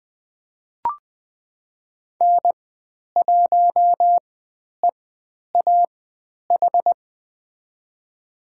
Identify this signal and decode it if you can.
Morse code, heard as a keyed tone, about 20 wpm, spaced as for 10 wpm, 700 Hz. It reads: N1EAH
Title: 102 Lesson 17g - Letter u - Call Signs 20wpm